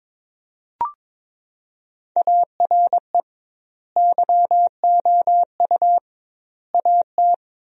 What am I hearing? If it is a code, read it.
Morse code, heard as a keyed tone, about 22 wpm, 700 Hz. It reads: ARE YOU AT
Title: 2x Word-Spacing - Sentences from Top 200 Words - 22wpm